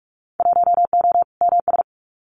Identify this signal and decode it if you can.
Morse code, heard as a keyed tone, about 44 wpm, 700 Hz. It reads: 1O MS
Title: Winter Field Day Exchanges 45wpm